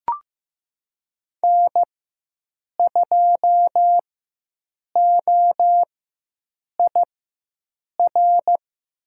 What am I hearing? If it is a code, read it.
Morse code, heard as a keyed tone, about 15 wpm, spaced as for 7 wpm, 700 Hz. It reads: N2OIR
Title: USA Call Signs - 5 Characters - 4x Character Spacing - 15wpm